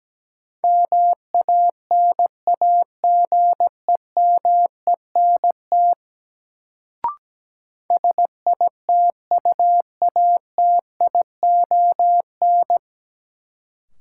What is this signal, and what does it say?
Morse code, heard as a keyed tone, about 17 wpm, 700 Hz. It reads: MANAGEMENT SITUATION
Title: Sentences from Top 1200 Words - New Word Drill Alt - 17wpm